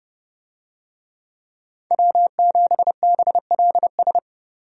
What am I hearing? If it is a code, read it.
Morse code, heard as a keyed tone, about 30 wpm, 700 Hz. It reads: W7BLS
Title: USA Call Signs - 5 Characters - 30wpm